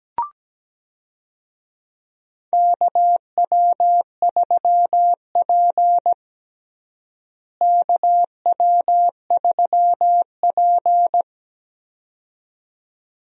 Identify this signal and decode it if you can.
Morse code, heard as a keyed tone, about 17 wpm, 700 Hz. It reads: KW3P KW3P
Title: USA Call Signs - 4 Characters - Repeated 2x - 17wpm